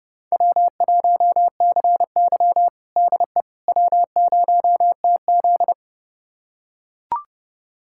Morse code, heard as a keyed tone, about 29 wpm, 700 Hz. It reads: W1CY DE W0TZ
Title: USA Callsign DE Callsign - 4 Characters - 30wpm